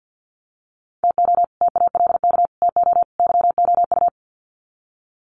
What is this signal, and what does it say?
Morse code, heard as a keyed tone, about 50 wpm, 700 Hz. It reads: TO TALK TO YOU
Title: Sets of 4 Words 50wpm